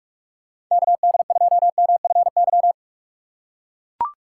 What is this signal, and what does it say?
Morse code, heard as a keyed tone, about 41 wpm, 700 Hz. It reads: KD2MFY
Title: USA Call Signs - 6 Characters - 6x Character Spacing - 45wpm